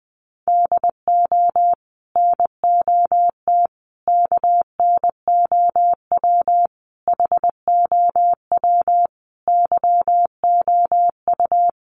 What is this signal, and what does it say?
Morse code, heard as a keyed tone, about 20 wpm, 700 Hz. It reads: DO NOT KNOW HOW YOU
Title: Sets of 5 Words Encore 20wpm